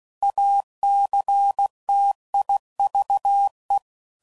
Morse code, heard as a keyed tone, about 16 wpm, 800 Hz. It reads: ACTIVE
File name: CW Code Groups @15WPM / Technical Subjects / AND Gate